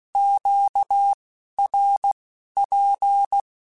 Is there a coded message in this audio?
Yes: Morse code from 0.2 to 3.4 s, heard as a keyed tone, about 16 wpm, spaced as for 11 wpm, 800 Hz. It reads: QRP